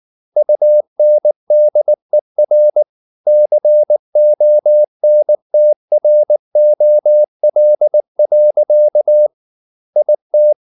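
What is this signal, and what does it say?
Morse code, heard as a keyed tone, about 19 wpm, 600 Hz. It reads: UNDER CONTROL. IT
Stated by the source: Morse code at 19 wpm from CW med Gnister, reading Ken Follett's novel Never in English.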